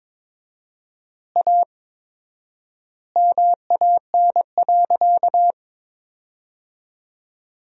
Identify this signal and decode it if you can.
Morse code, heard as a keyed tone, about 22 wpm, 700 Hz. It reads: A MAN.
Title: 4x Word-Spacing - Sentences from Top 300 Words - 22wpm